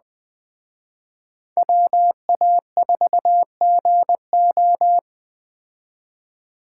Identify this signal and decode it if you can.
Morse code, heard as a keyed tone, about 20 wpm, 700 Hz. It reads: WA4GO